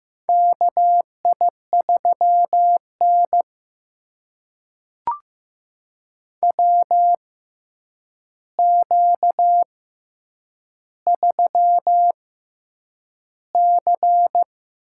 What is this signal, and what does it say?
Morse code, heard as a keyed tone, about 15 wpm, spaced as for 8 wpm, 700 Hz. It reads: KI3N WQ3C
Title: USA Call Signs - 4 Characters - 6x Character Spacing - 15wpm